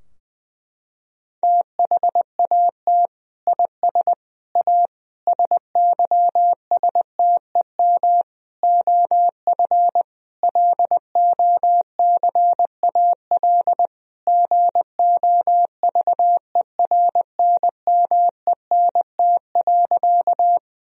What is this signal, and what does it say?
Morse code, heard as a keyed tone, about 21 wpm, 700 Hz. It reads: THAT IS A SYSTEM OF LOCAL GOVERNMENT.
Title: Sentences from Top 1000 Words - Review Part 3 - 20wpm